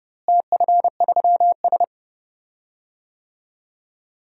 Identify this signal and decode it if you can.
Morse code, heard as a keyed tone, about 31 wpm, 700 Hz. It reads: TF3S
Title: DXCC-100 Call Signs - 4 Characters - Repeated 2x - 30wpm